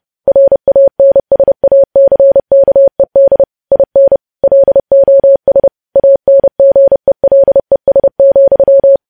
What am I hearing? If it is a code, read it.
Morse code, heard as a keyed tone, about 30 wpm, 550 Hz. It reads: RANSACKED IN LOS ANGELES,
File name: podcast / News headlines-1x-30x30